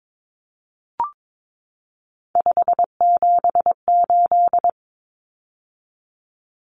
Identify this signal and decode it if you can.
Morse code, heard as a keyed tone, about 22 wpm, 700 Hz. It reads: 578
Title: Numbers - 3-Digit 22wpm